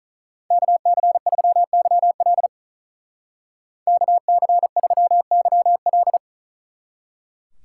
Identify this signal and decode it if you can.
Morse code, heard as a keyed tone, about 45 wpm, 700 Hz. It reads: KC3YL KC3YL